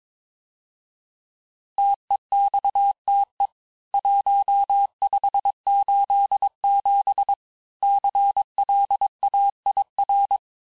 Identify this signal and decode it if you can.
Morse code, heard as a keyed tone, about 22 wpm, 800 Hz. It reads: TEXTE 1587 CLAIR
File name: cours-du-lundi files / cours-1587-texte-clair-vitesse-16mpm-puis-21mpm